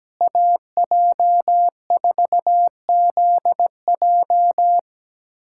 Morse code, heard as a keyed tone, about 17 wpm, 700 Hz. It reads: AJ4ZJ
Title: USA Callsign DE Callsign - 5 Characters - 17wpm